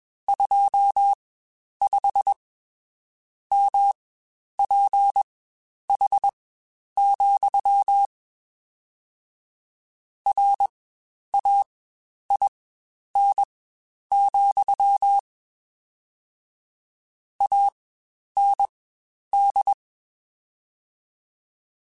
Morse code, heard as a keyed tone, about 21 wpm, spaced as for 9 wpm, 800 Hz. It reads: 25MPH, RAIN, AND
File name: Spaced Characters / Spaced Characters-mp3s / Weather5 5 sp @20